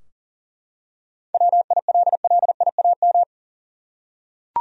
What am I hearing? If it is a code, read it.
Morse code, heard as a keyed tone, about 40 wpm, 700 Hz. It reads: WILLIAM
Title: USA Common Names - 40wpm